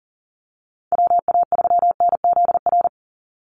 Morse code, heard as a keyed tone, about 40 wpm, 700 Hz. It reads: WA3NZR
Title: Call Signs Repeated 3x - 40wpm